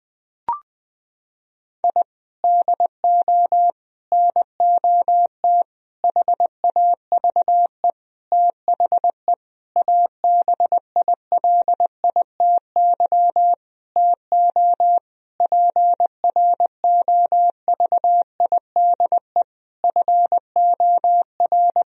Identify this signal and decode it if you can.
Morse code, heard as a keyed tone, about 20 wpm, 700 Hz. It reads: I DO NOT HAVE THE ABILITY TO PROVIDE FOR
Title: Sentences from Top 1000 Words - Review Part 3 - 20wpm